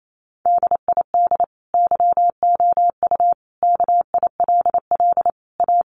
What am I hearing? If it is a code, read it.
Morse code, heard as a keyed tone, about 28 wpm, 700 Hz. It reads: DID YOU KILL A
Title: Sentences from Top 1400 Words 28wpm